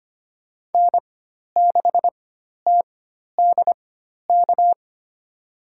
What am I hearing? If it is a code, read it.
Morse code, heard as a keyed tone, about 25 wpm, spaced as for 12 wpm, 700 Hz. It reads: N6TDK
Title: USA Call Signs - 5 Characters - 4x Character Spacing - 25wpm